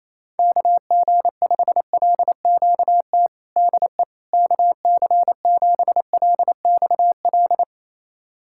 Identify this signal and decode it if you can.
Morse code, heard as a keyed tone, about 28 wpm, 700 Hz. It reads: KG5LQT DE KC7LXL
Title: USA Callsign DE Callsign - 6 Characters - 28wpm